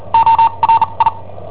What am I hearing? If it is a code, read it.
Morse code, heard as a keyed tone, about 43 wpm, spaced as for 32 wpm, 910 Hz. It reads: ORI
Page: Cassetta esercizi preparata da I7OHP per IW5ELP